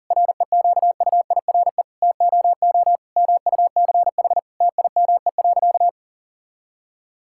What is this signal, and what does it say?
Morse code, heard as a keyed tone, about 40 wpm, 700 Hz. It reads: REQUIRE TOO MUCH TIME.
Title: Sentences from Top 1000 Words - Review Part 6 - 40wpm